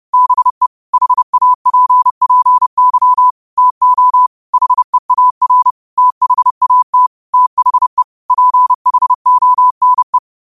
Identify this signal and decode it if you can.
Morse code, heard as a keyed tone, about 30 wpm, 1000 Hz. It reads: BE HAPPY TO HEAR THAT THE PHONE
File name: qotdfiles / 30wpm